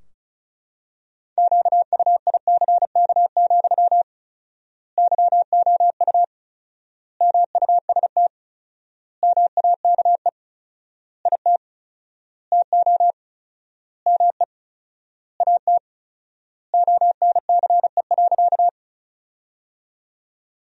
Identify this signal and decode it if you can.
Morse code, heard as a keyed tone, about 34 wpm, 700 Hz. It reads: QUICK, YOU MUST MAKE IT TO ME AT ONCE.
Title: 4x Word-Spacing - Sentences from Top 500 Words - 35wpm